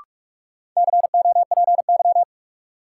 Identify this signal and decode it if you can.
Morse code, heard as a keyed tone, about 42 wpm, 700 Hz. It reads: COPY
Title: Sentences from Top 1400 Words - New Word Drill Alt - 45wpm